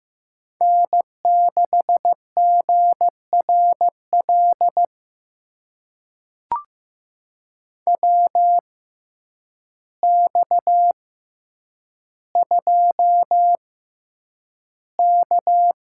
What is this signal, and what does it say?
Morse code, heard as a keyed tone, about 15 wpm, spaced as for 8 wpm, 700 Hz. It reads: N6GRL WX2K
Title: USA Call Signs - 6x Character Spacing - 15wpm